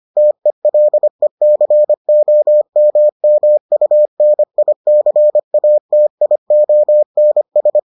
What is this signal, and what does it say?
Morse code, heard as a keyed tone, about 25 wpm, 600 Hz. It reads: TELECOMMUNICATIONS